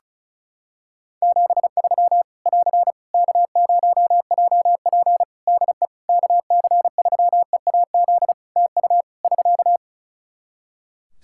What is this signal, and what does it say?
Morse code, heard as a keyed tone, about 34 wpm, 700 Hz. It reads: 73 <AR> K0JP DE KC3EAZ TU <SK>
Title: Practice QSO - Exchange 4 - Ending - 35wpm